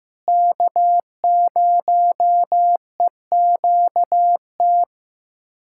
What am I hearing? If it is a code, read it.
Morse code, heard as a keyed tone, about 15 wpm, 700 Hz. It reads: K0EQT
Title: USA Callsign DE Callsign - 5 Characters - 15wpm